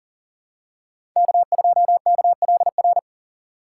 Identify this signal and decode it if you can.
Morse code, heard as a keyed tone, about 40 wpm, 700 Hz. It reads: K2KLR
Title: USA Call Signs - 5 Characters - 3x Character Spacing - 40wpm